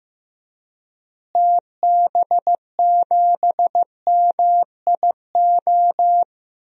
Morse code, heard as a keyed tone, about 15 wpm, 700 Hz. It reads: TB7MIO
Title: DXCC-100 Call Signs - 6 Characters - Repeated 2x - 15wpm